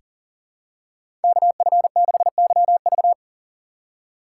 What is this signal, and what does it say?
Morse code, heard as a keyed tone, about 40 wpm, 700 Hz. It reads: KF6YV
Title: USA Call Signs - 5 Characters - 4x Character Spacing - 40wpm